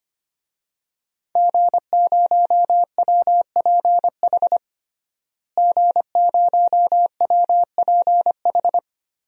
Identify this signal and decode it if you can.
Morse code, heard as a keyed tone, about 25 wpm, 700 Hz. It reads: G0WPH G0WPH
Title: DXCC-100 Call Signs - 5 Characters - Repeated 3x - 25wpm